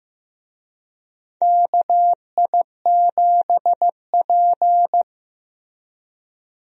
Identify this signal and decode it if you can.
Morse code, heard as a keyed tone, about 15 wpm, 700 Hz. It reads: KI7P
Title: USA Call Signs - 4 Characters - Repeated 2x - 15wpm